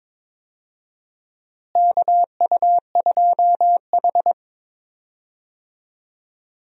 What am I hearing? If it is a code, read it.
Morse code, heard as a keyed tone, about 22 wpm, 700 Hz. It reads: KU2H
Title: USA Call Signs - 4 Characters - Repeated 3x - 22wpm